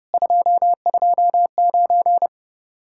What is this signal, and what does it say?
Morse code, heard as a keyed tone, about 30 wpm, 700 Hz. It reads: 229